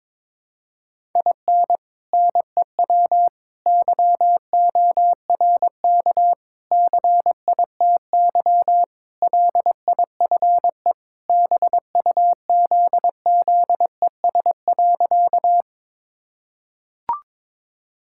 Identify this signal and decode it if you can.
Morse code, heard as a keyed tone, about 22 wpm, 700 Hz. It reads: IN NEW YORK CITY LIFE BUZZES.
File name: practice / 22wpm / Sentences with 4-Word Phrases - 22wpm